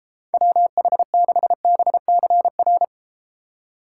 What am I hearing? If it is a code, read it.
Morse code, heard as a keyed tone, about 33 wpm, 700 Hz. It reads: WH6BCR